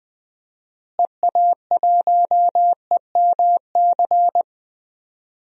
Morse code, heard as a keyed tone, about 20 wpm, 700 Hz. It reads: EA1EMC
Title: DXCC-100 Call Signs - 6 Characters - Repeated 3x - 20wpm